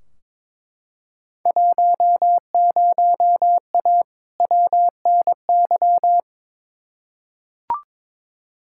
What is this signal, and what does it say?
Morse code, heard as a keyed tone, about 21 wpm, 700 Hz. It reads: 10A WNY
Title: Field Day Exchanges - 22wpm